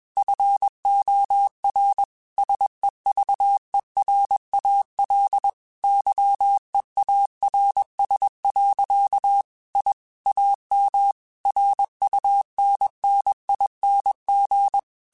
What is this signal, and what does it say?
Morse code, heard as a keyed tone, about 21 wpm, 800 Hz. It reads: FOR SEVERAL YEARS. I AM RUNNING
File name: QSOs / QSOs-mp3s / kc7sac@20